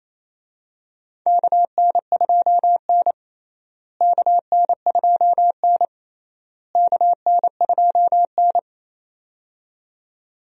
Morse code, heard as a keyed tone, about 28 wpm, 700 Hz. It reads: KN2N KN2N KN2N
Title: USA Call Signs - 4 Characters - Repeated 3x - 28wpm